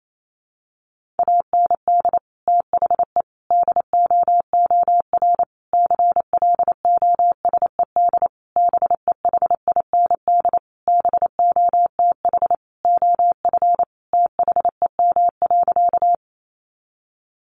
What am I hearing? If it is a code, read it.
Morse code, heard as a keyed tone, about 28 wpm, 700 Hz. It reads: AND THE DOOR CLOSED BEHIND BOTH OF THEM.
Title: Sentences from Top 1400 Words 28wpm